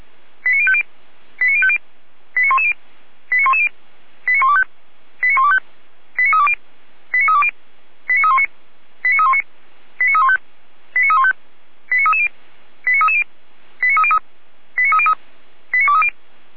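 Five-tone selective call (ZVEI), heard as quick runs of five tones, tones of about 70 ms. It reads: 89050 89050 89110 89110 89125 89125 89320 89320 89319 89319 89315 89315 89330 89330 89303 89303 89230
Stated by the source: ZVEI five-tone selective call 89097 received on SDR.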